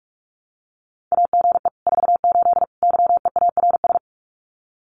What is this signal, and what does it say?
Morse code, heard as a keyed tone, about 46 wpm, 700 Hz. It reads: AGE 48 YEARS
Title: Common QSO Phrases Encore 45wpm